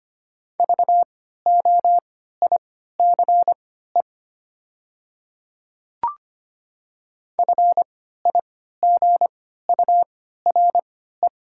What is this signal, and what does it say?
Morse code, heard as a keyed tone, about 25 wpm, spaced as for 14 wpm, 700 Hz. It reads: VOICE FIGURE
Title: 3x Character Spacing - Top 500 Words - 25wpm